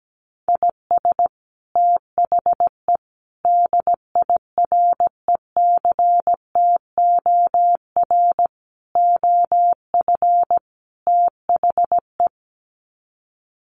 Morse code, heard as a keyed tone, about 17 wpm, 700 Hz. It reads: IS THE DIRECTOR OF THE
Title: Sets of 5 Words Encore2 17wpm